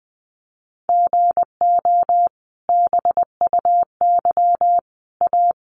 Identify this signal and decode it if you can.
Morse code, heard as a keyed tone, about 20 wpm, 700 Hz. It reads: GO BUY A
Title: Sentences from Top 1200 Words 20wpm